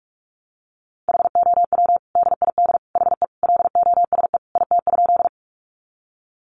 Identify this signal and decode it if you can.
Morse code, heard as a keyed tone, about 44 wpm, 700 Hz. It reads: HOW DID HE LOSE IT?
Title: Sentences from Top 1100 Words 45wpm